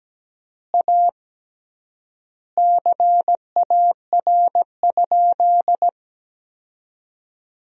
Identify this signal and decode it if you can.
Morse code, heard as a keyed tone, about 17 wpm, 700 Hz. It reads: A CAR?
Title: 3x Word-Spacing - Sentences from Top 300 Words - 17wpm